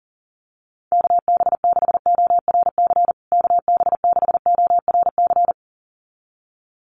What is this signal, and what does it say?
Morse code, heard as a keyed tone, about 40 wpm, 700 Hz. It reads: KB6ORC KB6ORC
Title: USA Call Signs Repeated 2x 40wpm